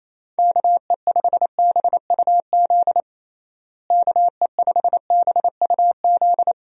Morse code, heard as a keyed tone, about 28 wpm, 700 Hz. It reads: KE5BUZ KE5BUZ
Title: USA Call Signs - 6 Characters - Repeated 2x - 28wpm